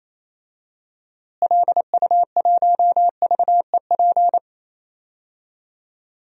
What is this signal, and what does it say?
Morse code, heard as a keyed tone, about 28 wpm, 700 Hz. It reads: LU1VEP
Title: DXCC-100 Call Signs - 6 Characters - Repeated 2x - 28wpm